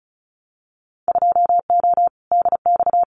Morse code, heard as a keyed tone, about 34 wpm, 700 Hz. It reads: 2O DX
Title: Winter Field Day Exchanges Encore 35wpm